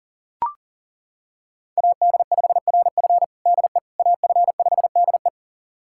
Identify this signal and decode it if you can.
Morse code, heard as a keyed tone, about 40 wpm, 690 Hz. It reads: AD5RF DE AF5DE
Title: USA Callsign DE Callsign - 5 Characters - 40wpm